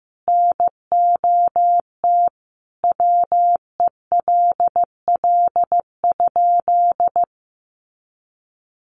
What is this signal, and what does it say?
Morse code, heard as a keyed tone, about 15 wpm, 700 Hz. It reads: NOT WELL?
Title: Sentences from Top 600 Words 15wpm